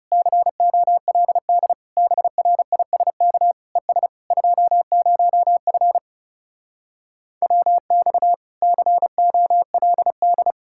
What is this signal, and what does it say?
Morse code, heard as a keyed tone, about 40 wpm, 700 Hz. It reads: COLD BRISK ES 20F WX COLD